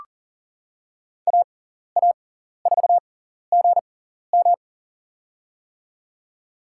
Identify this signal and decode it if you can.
Morse code, heard as a keyed tone, about 40 wpm, spaced as for 14 wpm, 700 Hz. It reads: AA4GM
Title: USA Call Signs - 5 Characters - 6x Character Spacing - 40wpm